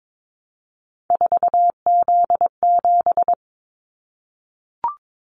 Morse code, heard as a keyed tone, about 22 wpm, 700 Hz. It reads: 4Z7
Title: Callsign Prefixes for Top 50 DXCC - 22wpm